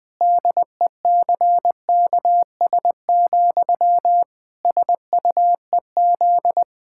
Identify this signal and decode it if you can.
Morse code, heard as a keyed tone, about 21 wpm, 700 Hz. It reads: DECKS, SUEZ
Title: Pangram Challenge v2 - 20wpm